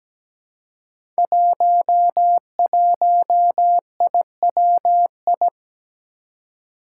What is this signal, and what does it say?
Morse code, heard as a keyed tone, about 17 wpm, 700 Hz. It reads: 11IWI
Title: Five Challenging Letter-Number Combinations 17wpm